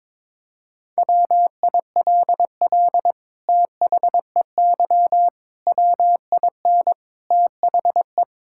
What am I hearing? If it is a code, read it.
Morse code, heard as a keyed tone, about 22 wpm, 700 Hz. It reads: WILL THEY WIN THE
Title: Sentences from Top 1600 Words 22wpm